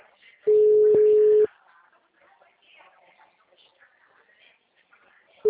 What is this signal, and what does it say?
Ringback tone (Europe, 425 Hz), with speech.